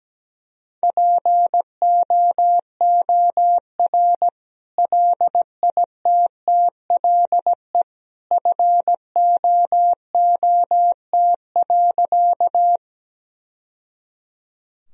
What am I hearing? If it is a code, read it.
Morse code, heard as a keyed tone, about 17 wpm, 700 Hz. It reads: POOR LITTLE FOOT.
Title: Sentences from Top 1000 Words - Review Part 3 - 17wpm